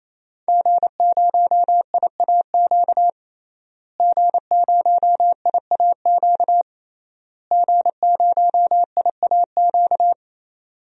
Morse code, heard as a keyed tone, about 28 wpm, 700 Hz. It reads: G0IAQ G0IAQ G0IAQ
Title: DXCC-100 Call Signs - 5 Characters - Repeated 3x - 28wpm